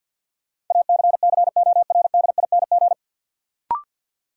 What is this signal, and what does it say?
Morse code, heard as a keyed tone, about 53 wpm, 700 Hz. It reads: ACCORDING
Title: Sets of 2 Words - New Word Drill - 50wpm